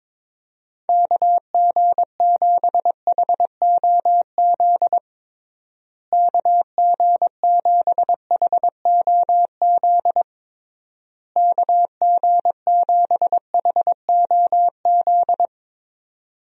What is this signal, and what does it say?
Morse code, heard as a keyed tone, about 22 wpm, 700 Hz. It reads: KG7HOZ KG7HOZ KG7HOZ
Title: USA Call Signs - 6 Characters - Repeated 3x - 22wpm